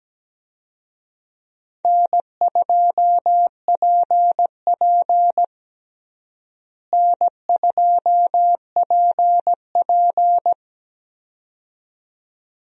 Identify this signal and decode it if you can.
Morse code, heard as a keyed tone, about 17 wpm, 700 Hz. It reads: N2PP N2PP
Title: USA Call Signs - 4 Characters - Repeated 2x - 17wpm